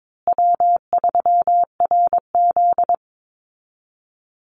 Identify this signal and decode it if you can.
Morse code, heard as a keyed tone, about 22 wpm, 700 Hz. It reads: W3RZ
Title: Easy Call Signs Repeated 3x - 22wpm